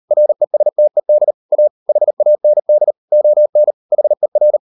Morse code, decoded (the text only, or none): RESTED A HAND ON HER